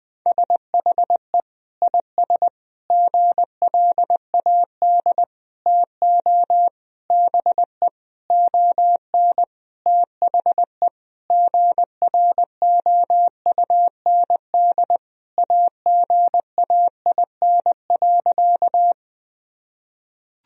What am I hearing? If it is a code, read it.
Morse code, heard as a keyed tone, about 21 wpm, 700 Hz. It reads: SHE IS GLAD TO BE ON THE GROUND AGAIN.
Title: Sentences from Top 1000 Words - Review Part 2 - 20wpm